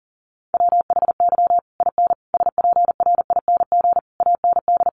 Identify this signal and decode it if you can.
Morse code, heard as a keyed tone, about 40 wpm, 700 Hz. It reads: WHY IN SPRING AND
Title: Sentences from Top 1200 Words 40wpm